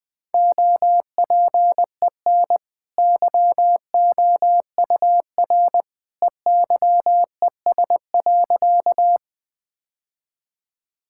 Morse code, heard as a keyed tone, about 20 wpm, 700 Hz. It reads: OPEN YOUR EYES.
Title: Easy Sentences with Hints - 20wpm